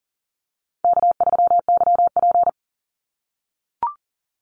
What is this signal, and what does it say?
Morse code, heard as a keyed tone, about 40 wpm, 700 Hz. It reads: K3YP
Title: Easy Call Signs - 40wpm